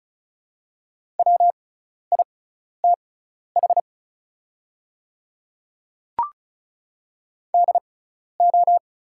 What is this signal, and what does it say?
Morse code, heard as a keyed tone, about 36 wpm, spaced as for 12 wpm, 700 Hz. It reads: WITH DO
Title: 6x Character Spacing - Top 100 Words - 35wpm